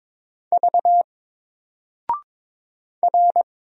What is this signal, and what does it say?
Morse code, heard as a keyed tone, about 22 wpm, 700 Hz. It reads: V R
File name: Single-Letter-Number-22wpm